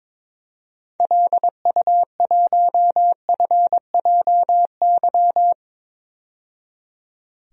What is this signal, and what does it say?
Morse code, heard as a keyed tone, about 22 wpm, 700 Hz. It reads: LU1FJY
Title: DXCC-100 Call Signs - 6 Characters - 22wpm